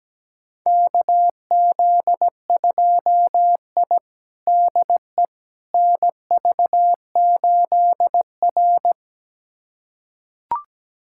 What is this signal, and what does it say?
Morse code, heard as a keyed tone, about 17 wpm, 700 Hz. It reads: KZ2I DE NV8R
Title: USA Callsign DE Callsign - 4 Characters - 17wpm